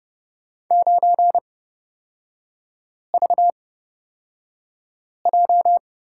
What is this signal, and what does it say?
Morse code, heard as a keyed tone, about 30 wpm, spaced as for 5 wpm, 700 Hz. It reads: 9VJ
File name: Lesson 38e - Letter q - Three Characters 30wpm